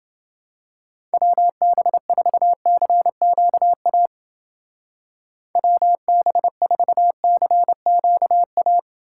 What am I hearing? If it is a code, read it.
Morse code, heard as a keyed tone, about 29 wpm, 700 Hz. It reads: WB4CQA WB4CQA